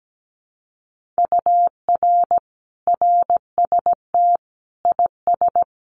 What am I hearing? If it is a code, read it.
Morse code, heard as a keyed tone, about 17 wpm, 700 Hz. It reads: UR RST IS
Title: Common QSO Phrases Encore 17wpm